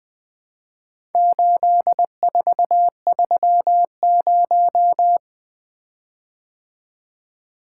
Morse code, heard as a keyed tone, about 20 wpm, 700 Hz. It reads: 8430
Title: Four Letter-Number Combinations - 20wpm